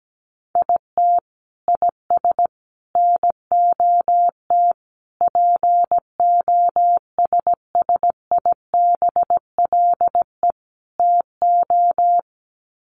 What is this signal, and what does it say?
Morse code, heard as a keyed tone, about 17 wpm, 700 Hz. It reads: IT IS NOT POSSIBLE TO
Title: Sets of 5 Words Encore2 17wpm